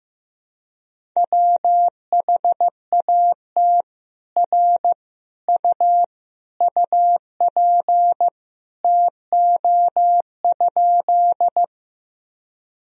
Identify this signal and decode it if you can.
Morse code, heard as a keyed tone, about 15 wpm, 700 Hz. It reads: WHAT R U UP TO?
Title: Sentences from Top 100 Words and Abbreviations 15wpm